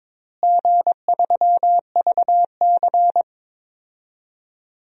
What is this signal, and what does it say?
Morse code, heard as a keyed tone, about 22 wpm, 700 Hz. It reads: G3VC
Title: DXCC-100 Call Signs - 4 Characters - Repeated 3x - 22wpm